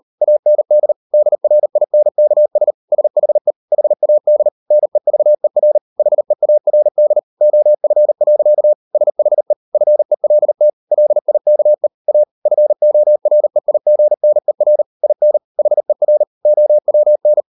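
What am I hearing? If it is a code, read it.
Morse code, heard as a keyed tone, about 38 wpm, 600 Hz. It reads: AND DRINKS SHE HAD NEVER HEARD OF. SHE FELT LIKE A FOREIGNER IN HER OWN
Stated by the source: Morse code practice at 38 WPM, sending Ken Follett's novel Never in English, from CW med Gnister.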